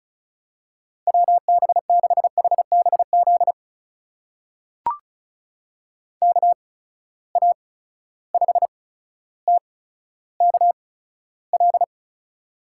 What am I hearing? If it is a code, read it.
Morse code, heard as a keyed tone, about 36 wpm, spaced as for 15 wpm, 700 Hz. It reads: WB6HBZ KA5TKL
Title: USA Call Signs - 6 Characters - 8x Character Spacing - 35wpm